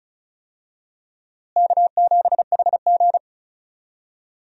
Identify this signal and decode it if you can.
Morse code, heard as a keyed tone, about 36 wpm, 700 Hz. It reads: K7HG
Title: USA Call Signs - 4 Characters - Repeated 2x - 35wpm